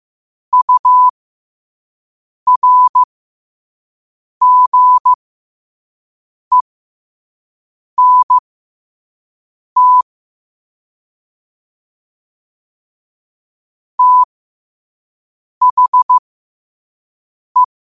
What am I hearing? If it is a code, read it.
Morse code, heard as a keyed tone, about 15 wpm, spaced as for 5 wpm, 1000 Hz. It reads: URGENT THE